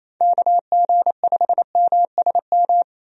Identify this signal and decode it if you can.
Morse code, heard as a keyed tone, about 27 wpm, 700 Hz. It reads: KG5MSM
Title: USA Call Signs - 6 Characters - 28wpm